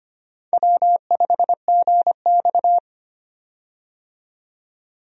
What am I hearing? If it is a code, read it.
Morse code, heard as a keyed tone, about 25 wpm, 700 Hz. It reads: W5GX
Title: USA Call Signs - 4 Characters - 25wpm